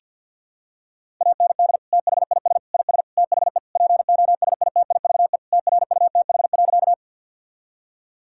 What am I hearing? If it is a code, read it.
Morse code, heard as a keyed tone, about 52 wpm, 690 Hz. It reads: AND THIS IS THE POSITIVE TRUTH.
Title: Sentences from Top 1000 Words - Review Part 5 - 50wpm